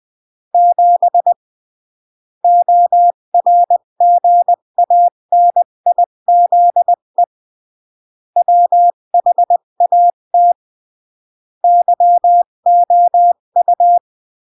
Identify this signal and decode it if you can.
Morse code, heard as a keyed tone, about 20 wpm, 700 Hz. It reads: 7 ORGANIZE WHAT YOU
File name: CW / jodorowsky / 82 commandments 20-15